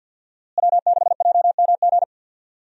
Morse code, heard as a keyed tone, about 47 wpm, 700 Hz. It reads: W6JMG
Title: USA Call Signs - 5 Characters - Repeated 2x - 50wpm